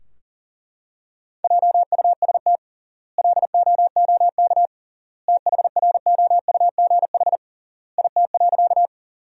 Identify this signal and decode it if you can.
Morse code, heard as a keyed tone, about 40 wpm, 700 Hz. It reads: JUST LOOK THROUGH IT.
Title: 3x Word-Spacing - Sentences from Top 200 Words - 40wpm